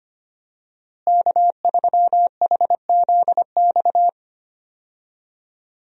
Morse code, heard as a keyed tone, about 25 wpm, 700 Hz. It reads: K3HZX
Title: USA Call Signs - 5 Characters - Repeated 2x - 25wpm